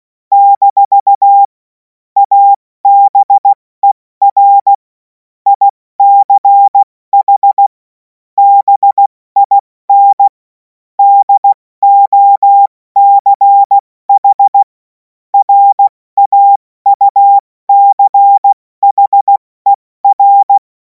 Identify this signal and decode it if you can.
Morse code, heard as a keyed tone, about 16 wpm, spaced as for 14 wpm, 800 Hz. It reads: - ABER ICH BIN DOCH RAUCHER
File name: Morsekurs / Witze / Witz 16. Juni 2026 70 F80